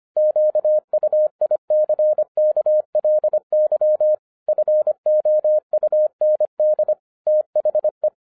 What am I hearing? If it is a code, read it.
Morse code, heard as a keyed tone, about 25 wpm, 610 Hz. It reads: QUICKLY FOUND THE
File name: FIRST FISTS CONTACT-25wpm